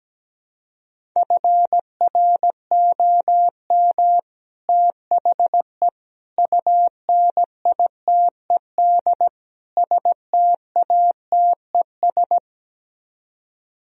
Morse code, heard as a keyed tone, about 17 wpm, 700 Hz. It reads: FROM THE UNITED STATES
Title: Speed-Racing - Sets of 4 Words 17wpm